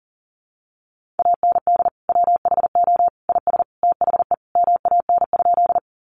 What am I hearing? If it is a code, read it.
Morse code, heard as a keyed tone, about 40 wpm, 700 Hz. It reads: AND WHO IS THE MAN?
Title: Sentences from Top 200 Words 40wpm